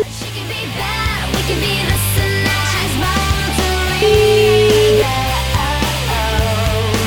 Ringback tone (Europe, 425 Hz), with noise.